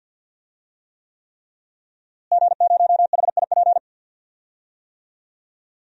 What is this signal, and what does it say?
Morse code, heard as a keyed tone, about 52 wpm, 700 Hz. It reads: G0HIP
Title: DXCC-100 Call Signs - 5 Characters - Repeated 3x - 50wpm